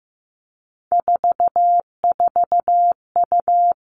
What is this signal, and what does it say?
Morse code, heard as a keyed tone, about 15 wpm, 700 Hz. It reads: 44U
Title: All Cut Numbers - 3-Digit 15wpm